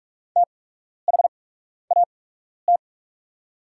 Morse code, heard as a keyed tone, about 43 wpm, spaced as for 12 wpm, 700 Hz. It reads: THAT
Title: 8x Character Spacing - Top 500 Words - 45wpm